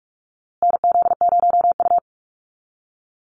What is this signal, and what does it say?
Morse code, heard as a keyed tone, about 44 wpm, 700 Hz. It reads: NZ0U
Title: USA Call Signs Repeated 3x 45wpm